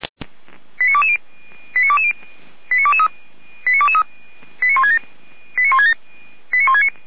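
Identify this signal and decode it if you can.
Five-tone selective call (ZVEI), heard as quick runs of five tones, tones of about 70 ms. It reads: 89220 89220 89223 89167 89167 89169
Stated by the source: ZVEI five-tone selective call received on SDR, code 89121.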